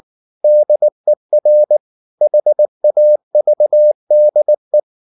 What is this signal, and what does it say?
Morse code, heard as a keyed tone, about 19 wpm, 600 Hz. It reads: DER HAVDE
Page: Ferie 19 wpm